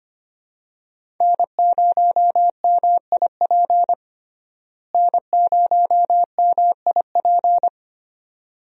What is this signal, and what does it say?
Morse code, heard as a keyed tone, about 25 wpm, 700 Hz. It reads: N0MIP N0MIP
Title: USA Call Signs - 5 Characters - Repeated 2x - 25wpm